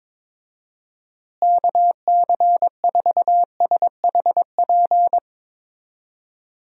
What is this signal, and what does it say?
Morse code, heard as a keyed tone, about 22 wpm, 700 Hz. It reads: KC4SHP
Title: USA Call Signs - 6 Characters - Repeated 3x - 22wpm